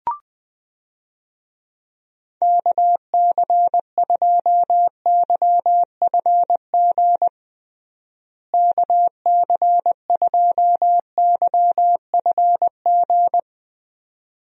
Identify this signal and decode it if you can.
Morse code, heard as a keyed tone, about 20 wpm, 700 Hz. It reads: KC2YFG KC2YFG
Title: USA Call Signs - 6 Characters - Repeated 2x - 20wpm